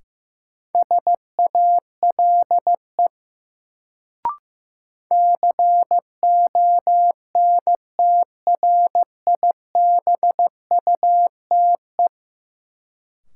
Morse code, heard as a keyed tone, about 15 wpm, 700 Hz. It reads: SALE CONTRIBUTE
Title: Sentences from Top 1400 Words - New Word Drill Alt - 15wpm